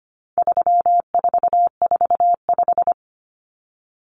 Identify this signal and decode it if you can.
Morse code, heard as a keyed tone, about 25 wpm, 700 Hz. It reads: 3445
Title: Numbers - 4-Digit 25wpm